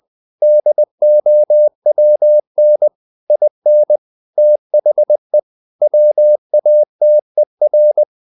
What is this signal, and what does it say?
Morse code, heard as a keyed tone, about 20 wpm, 600 Hz. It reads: DOWN IN THE WATER
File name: the old man and the sea part 2 20wpm